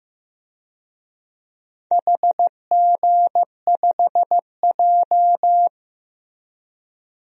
Morse code, heard as a keyed tone, about 15 wpm, 700 Hz. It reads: HG5J
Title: DXCC-100 Call Signs - 4 Characters - Repeated 3x - 15wpm